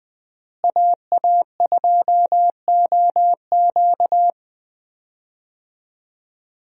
Morse code, heard as a keyed tone, about 20 wpm, 700 Hz. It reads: AA2OQ
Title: USA Call Signs - 5 Characters - Repeated 2x - 20wpm